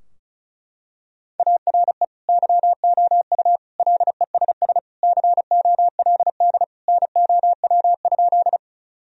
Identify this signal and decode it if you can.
Morse code, heard as a keyed tone, about 34 wpm, 700 Hz. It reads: ARE YOU LESS COLD NOW?
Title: Sentences from Top 1000 Words - Review Part 2 - 35wpm